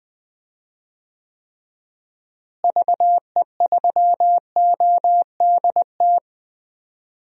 Morse code, heard as a keyed tone, about 20 wpm, 700 Hz. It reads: VE3ODT
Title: DXCC-100 Call Signs - 6 Characters - Repeated 3x - 20wpm